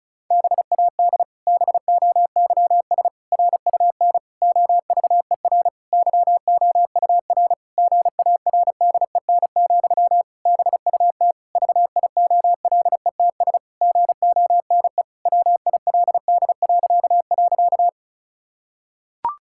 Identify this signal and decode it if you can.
Morse code, heard as a keyed tone, about 34 wpm, 700 Hz. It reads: BAD BOYS RUN OVER YOUR GARDEN, BUT VIOLETS GONE WILD..
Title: Pangram Challenge v2 - 35wpm